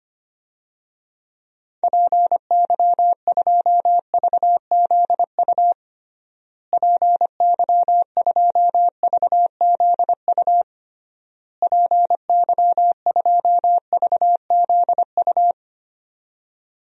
Morse code, heard as a keyed tone, about 25 wpm, 700 Hz. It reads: PY2VZU PY2VZU PY2VZU
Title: DXCC-100 Call Signs - 6 Characters - Repeated 3x - 25wpm